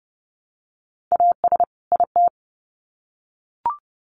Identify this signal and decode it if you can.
Morse code, heard as a keyed tone, about 30 wpm, 700 Hz. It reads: AS IT
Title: Sets of 2 Words 30wpm